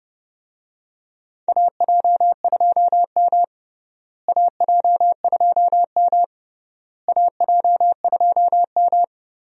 Morse code, heard as a keyed tone, about 30 wpm, 700 Hz. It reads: AJ2M AJ2M AJ2M
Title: USA Call Signs - 4 Characters - Repeated 3x - 30wpm